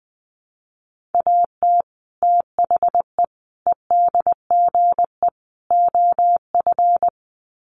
Morse code, heard as a keyed tone, about 20 wpm, 700 Hz. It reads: AT THE EDGE OF
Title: Sets of 4 Words 20wpm